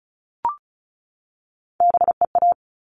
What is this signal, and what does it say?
Morse code, heard as a keyed tone, about 34 wpm, 700 Hz. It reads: BEA
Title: All Cut Numbers - 3-Digit 35wpm